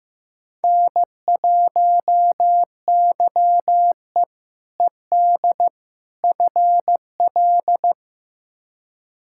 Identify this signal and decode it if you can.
Morse code, heard as a keyed tone, about 15 wpm, 700 Hz. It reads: N1YE ED FL
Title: K1USN SST QSO Practice - 15wpm